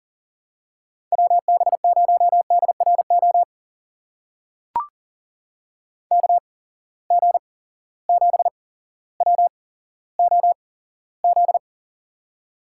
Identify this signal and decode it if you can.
Morse code, heard as a keyed tone, about 42 wpm, spaced as for 17 wpm, 700 Hz. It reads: WB0DRO KG7WOZ
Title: USA Call Signs - 6 Characters - 8x Character Spacing - 40wpm